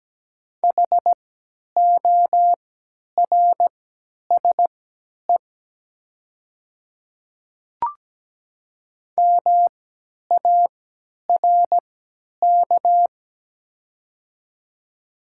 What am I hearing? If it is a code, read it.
Morse code, heard as a keyed tone, about 17 wpm, spaced as for 10 wpm, 700 Hz. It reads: HORSE MARK
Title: 3x Character Spacing - Top 300 Words - 17wpm